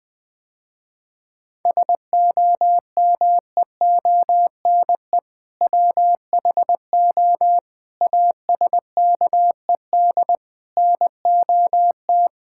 Morse code, heard as a keyed tone, about 21 wpm, 700 Hz. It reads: SOMEONE WHO ASKED NOT
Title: Sentences with 7-Word Phrases - Spoken Beginning - 20wpm